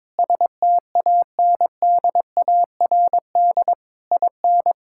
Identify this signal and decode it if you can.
Morse code, heard as a keyed tone, about 22 wpm, 700 Hz. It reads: STANDARD IN
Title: Sentences from Top 1000 Words - Review Part 6 - 22wpm